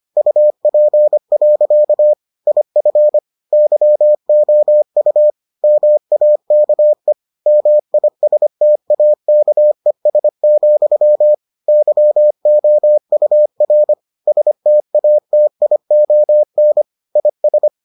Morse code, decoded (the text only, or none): UP. IF YOU MAKE MISTAKES, YOUR STATION IS